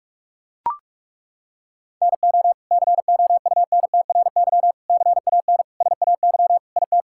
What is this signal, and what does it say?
Morse code, heard as a keyed tone, about 41 wpm, 700 Hz. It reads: NO COUNTRY CAN SAY IT
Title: Sentences from Top 1000 Words - Review Part 5 - 45wpm